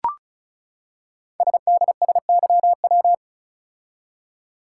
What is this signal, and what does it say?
Morse code, heard as a keyed tone, about 36 wpm, 700 Hz. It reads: SDSYW
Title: Code Groups - 5 Letter:Numbers - 35wpm